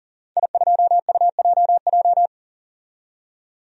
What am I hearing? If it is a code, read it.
Morse code, heard as a keyed tone, about 40 wpm, 700 Hz. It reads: I2UJJ
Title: DXCC-100 Call Signs - 5 Characters - 40wpm